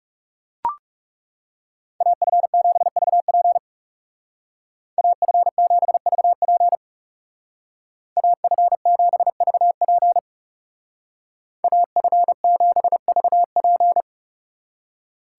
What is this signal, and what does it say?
Morse code, heard as a keyed tone, about 36 wpm, 700 Hz. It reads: AF7VP AF7VP AF7VP AF7VP
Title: Speed Racing - USA 2x2 Call Signs 30wpm